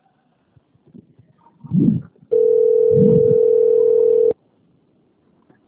Ringback tone (North America), 440 + 480 Hz.